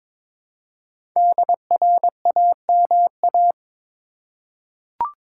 Morse code, heard as a keyed tone, about 22 wpm, 700 Hz. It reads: DRAMA
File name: 5 Letter Words - First 3 Letters Given - 22wpm